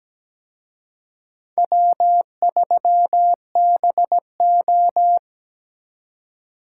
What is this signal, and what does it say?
Morse code, heard as a keyed tone, about 20 wpm, 700 Hz. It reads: W3BO
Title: Speed Racing - USA Call Signs 17wpm